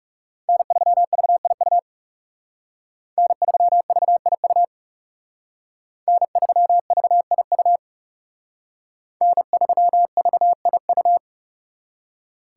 Morse code, heard as a keyed tone, about 37 wpm, 700 Hz. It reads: N3VIU N3VIU N3VIU N3VIU
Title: Speed Racing - USA 1x3 Call Signs 30wpm